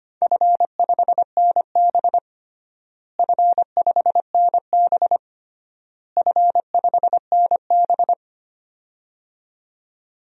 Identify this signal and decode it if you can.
Morse code, heard as a keyed tone, about 25 wpm, 700 Hz. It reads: F5NB F5NB F5NB
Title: DXCC-100 Call Signs - 4 Characters - Repeated 3x - 25wpm